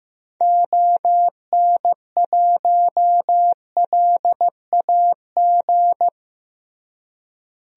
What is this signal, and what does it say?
Morse code, heard as a keyed tone, about 15 wpm, 700 Hz. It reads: ON1LAG
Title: DXCC-100 Call Signs - 6 Characters - Repeated 3x - 15wpm